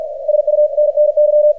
Morse code, heard as a keyed tone, about 48 wpm, 600 Hz, with radio noise. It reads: SUNNY
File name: archived / CRSnet